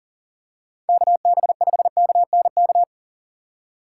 Morse code, heard as a keyed tone, about 40 wpm, 700 Hz. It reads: KB5KNK
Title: USA Call Signs - 2x Character Spacing - 40wpm